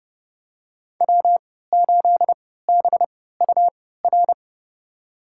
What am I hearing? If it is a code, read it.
Morse code, heard as a keyed tone, about 30 wpm, spaced as for 17 wpm, 700 Hz. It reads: W8BUR
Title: USA Call Signs - 5 Characters - 3x Character Spacing - 30wpm